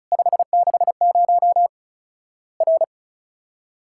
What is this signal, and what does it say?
Morse code, heard as a keyed tone, about 36 wpm, 700 Hz. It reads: 560 R
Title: Worked All Europe DX Contest - QTC Exchange - 35wpm